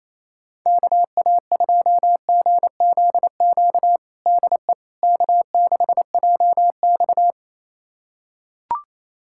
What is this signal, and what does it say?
Morse code, heard as a keyed tone, about 28 wpm, 700 Hz. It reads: KA2GZQ DE K6JX
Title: USA Callsign DE Callsign 28wpm